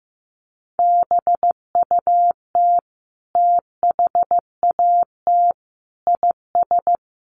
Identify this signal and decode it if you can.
Morse code, heard as a keyed tone, about 15 wpm, 700 Hz. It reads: BUT THAT IS
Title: Sentences from Top 600 Words 15wpm